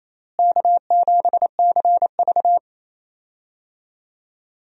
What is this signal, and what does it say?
Morse code, heard as a keyed tone, about 28 wpm, 700 Hz. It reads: K7CV
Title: USA Call Signs - 4 Characters - Repeated 3x - 28wpm